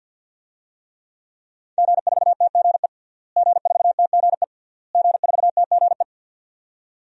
Morse code, heard as a keyed tone, about 50 wpm, 700 Hz. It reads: G4TGE G4TGE G4TGE
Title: DXCC-100 Call Signs - 5 Characters - Repeated 3x - 50wpm